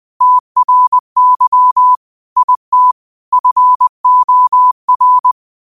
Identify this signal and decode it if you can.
Morse code, heard as a keyed tone, about 20 wpm, 1000 Hz. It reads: TRY IT FOR
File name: qotdfiles / 20wpm